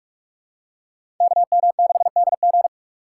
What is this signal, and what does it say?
Morse code, heard as a keyed tone, about 45 wpm, 700 Hz. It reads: KM6DG
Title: USA Call Signs - 5 Characters - 2x Character Spacing - 45wpm